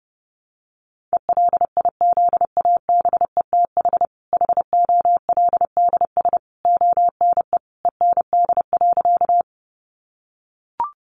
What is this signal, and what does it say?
Morse code, heard as a keyed tone, about 30 wpm, 700 Hz. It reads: ELIZABETH HOLDS ONE END.
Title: Easy Phrases 30wpm